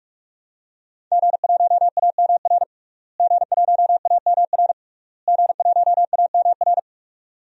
Morse code, heard as a keyed tone, about 46 wpm, 700 Hz. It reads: G1AMR G1AMR G1AMR
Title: DXCC-100 Call Signs - 5 Characters - Repeated 3x - 45wpm